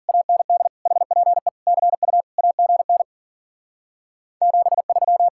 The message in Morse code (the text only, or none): AND HPE CU AGN 73